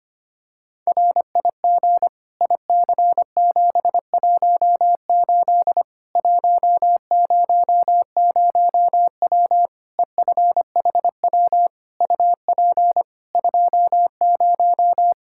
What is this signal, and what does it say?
Morse code, heard as a keyed tone, about 25 wpm, 700 Hz. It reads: RIG IC718 100W EFHW UP 20
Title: Practice QSO - Exchange 2 - Info - 25wpm